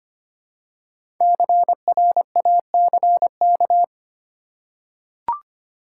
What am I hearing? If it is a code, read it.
Morse code, heard as a keyed tone, about 25 wpm, 700 Hz. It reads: CRACK
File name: 5 Letter Words - First 3 Letters Given - 25wpm